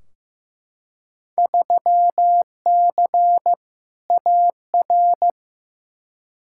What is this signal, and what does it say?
Morse code, heard as a keyed tone, about 15 wpm, 700 Hz. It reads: 3C AR
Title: Field Day Exchanges Encore - 15wpm